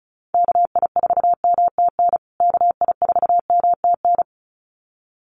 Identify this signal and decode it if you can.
Morse code, heard as a keyed tone, about 34 wpm, 700 Hz. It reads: KI4MTN KI4MTN
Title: USA Call Signs Repeated 2x 35wpm